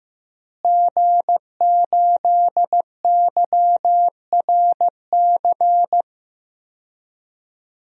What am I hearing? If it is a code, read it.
Morse code, heard as a keyed tone, about 15 wpm, 700 Hz. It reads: G8YRC
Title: DXCC-100 Call Signs - 5 Characters - Repeated 2x - 15wpm